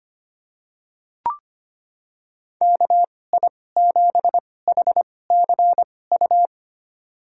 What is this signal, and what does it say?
Morse code, heard as a keyed tone, about 25 wpm, spaced as for 18 wpm, 700 Hz. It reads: KI7HCU
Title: USA Call Signs - 6 Characters - 2x Character Spacing - 25wpm